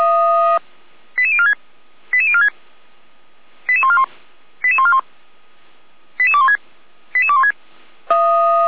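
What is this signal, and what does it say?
Five-tone selective call (ZVEI), heard as quick runs of five tones, tones of about 70 ms. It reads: 88046 88046 88141 88141 88216 88216
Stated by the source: received on an SDR; ZVEI five-tone call 88489, 455 directory.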